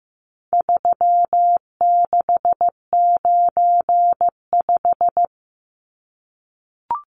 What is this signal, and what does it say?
Morse code, heard as a keyed tone, about 15 wpm, 700 Hz. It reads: 3695
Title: Numbers - 4-Digit 15wpm